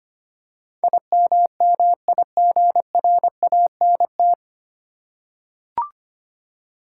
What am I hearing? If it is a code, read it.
Morse code, heard as a keyed tone, about 25 wpm, 700 Hz. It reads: IMMIGRANT
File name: Nine Letter Words - 25wpm